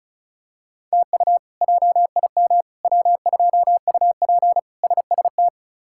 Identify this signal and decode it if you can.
Morse code, heard as a keyed tone, about 36 wpm, 700 Hz. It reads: TU JIM W2UP SST
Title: K1USN SST QSO Practice - 35wpm